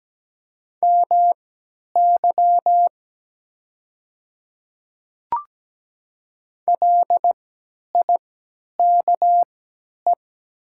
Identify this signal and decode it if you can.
Morse code, heard as a keyed tone, about 17 wpm, spaced as for 10 wpm, 700 Hz. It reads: MY LIKE
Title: 3x Character Spacing - Top 100 Words - 17wpm